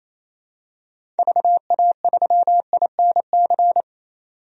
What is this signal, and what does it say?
Morse code, heard as a keyed tone, about 28 wpm, 700 Hz. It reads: VA3INC
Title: DXCC-100 Call Signs - 6 Characters - 28wpm